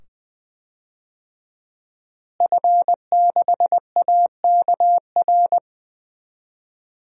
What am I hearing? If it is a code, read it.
Morse code, heard as a keyed tone, about 20 wpm, 700 Hz. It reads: F6AKR
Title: DXCC-100 Call Signs - 5 Characters - Repeated 2x - 20wpm